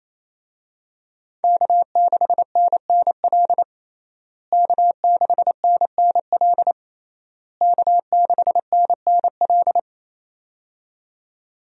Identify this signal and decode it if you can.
Morse code, heard as a keyed tone, about 28 wpm, 700 Hz. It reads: K6NNL K6NNL K6NNL
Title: USA Call Signs - 5 Characters - Repeated 3x - 28wpm